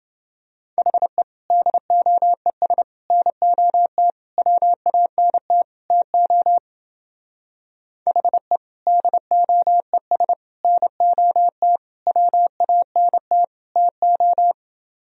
Morse code, heard as a keyed tone, about 28 wpm, 700 Hz. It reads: HE DOES NOT WANT TO HE DOES NOT WANT TO